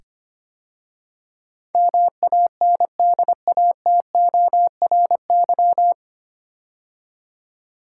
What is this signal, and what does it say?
Morse code, heard as a keyed tone, about 25 wpm, 700 Hz. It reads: MANDATORY
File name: practice / 25wpm / 9 Letter Words - First 3 Letters Given - 25wpm